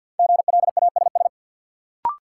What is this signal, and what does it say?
Morse code, heard as a keyed tone, about 53 wpm, 700 Hz. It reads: GLASS
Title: Sentences from Top 1000 Words - Review Part 3 - New Word Drill - 50wpm